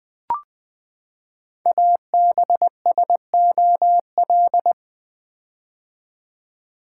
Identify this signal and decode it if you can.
Morse code, heard as a keyed tone, about 20 wpm, 700 Hz. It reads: ABSOL
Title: Build A Word - 5 Letters At A Time - 10-Letter Words - 20wpm